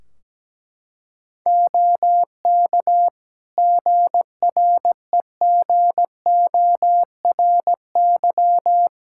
Morse code, heard as a keyed tone, about 17 wpm, 700 Hz. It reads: OK GREGORY
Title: Practice QSO - Exchange 3 - More Info - 17wpm